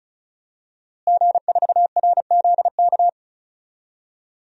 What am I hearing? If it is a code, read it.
Morse code, heard as a keyed tone, about 34 wpm, 700 Hz. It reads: G4RZK
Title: DXCC-100 Call Signs - 5 Characters - Repeated 3x - 35wpm